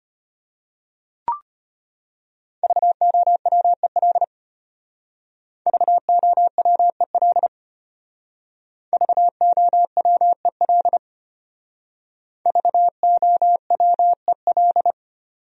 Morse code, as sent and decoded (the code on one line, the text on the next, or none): ...- --- .-- . .-.. / ...- --- .-- . .-.. / ...- --- .-- . .-.. / ...- --- .-- . .-..
VOWEL VOWEL VOWEL VOWEL